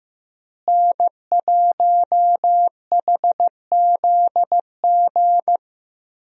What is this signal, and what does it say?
Morse code, heard as a keyed tone, about 16 wpm, 700 Hz. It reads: N1HZG
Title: USA Call Signs - 5 Characters - Repeated 3x - 15wpm